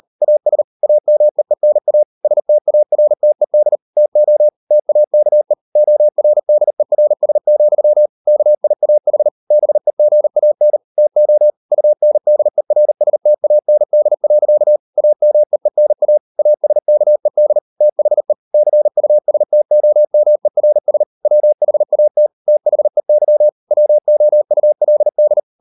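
Morse code, heard as a keyed tone, about 38 wpm, 600 Hz. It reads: AS AMEENA STARTED TO TAKE ORDERS, KIAH BEGAN TO UNDERSTAND. AMEENA ASKED THE CUSTOMERS WHAT THEY WOULD
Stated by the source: Morse code practice at 38 WPM, sending Ken Follett's novel Never in English, from CW med Gnister.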